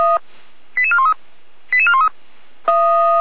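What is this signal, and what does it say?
Five-tone selective call (ZVEI), heard as quick runs of five tones, tones of about 70 ms. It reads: 88413 88413
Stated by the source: ZVEI five-tone selective call received by SDR, ID 88169, channel 455.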